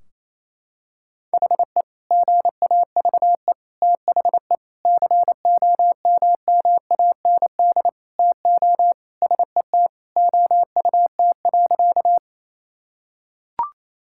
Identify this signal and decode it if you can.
Morse code, heard as a keyed tone, about 28 wpm, 700 Hz. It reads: HE GAVE THE COMMAND TO SET OUT.
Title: Sentences from Top 1600 Words 28wpm